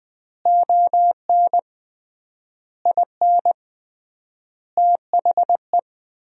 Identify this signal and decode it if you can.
Morse code, heard as a keyed tone, about 20 wpm, 700 Hz. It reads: ON IN THE
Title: 3x Word-Spacing - Sentences from Top 200 Words - 20wpm